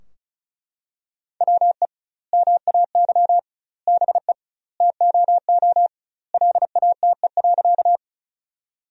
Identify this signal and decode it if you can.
Morse code, heard as a keyed tone, about 36 wpm, 700 Hz. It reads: WE MAY BE TOO LATE.
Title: 2x Word-Spacing - Sentences from Top 300 Words - 35wpm